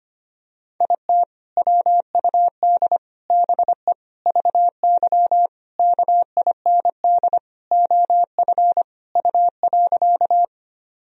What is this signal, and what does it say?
Morse code, heard as a keyed tone, about 25 wpm, 700 Hz. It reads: IT WUD BE VY KIND OF U.
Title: Sentences from Top 200 Words and Abbreviations 25wpm